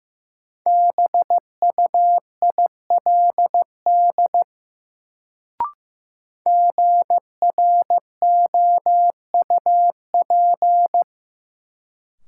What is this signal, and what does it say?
Morse code, heard as a keyed tone, about 15 wpm, 700 Hz. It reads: BUILD GROUP
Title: Sentences from Top 1000 Words - Review Part 1 - New Word Drill - 15wpm